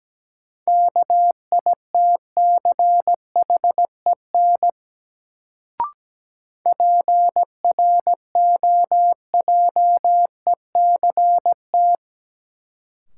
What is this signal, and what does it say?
Morse code, heard as a keyed tone, about 17 wpm, 700 Hz. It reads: KITCHEN PROJECT
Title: Sentences from Top 1400 Words - New Word Drill Alt - 17wpm